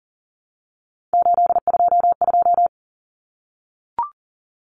Morse code, heard as a keyed tone, about 40 wpm, 700 Hz. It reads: 822
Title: Common Cut Numbers - 3-Digit 40wpm